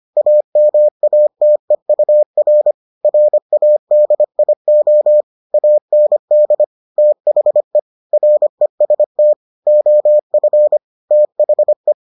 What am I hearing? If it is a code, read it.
Morse code, heard as a keyed tone, about 25 wpm, 600 Hz. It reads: AMATEUR RADIO AND THE REST OF THE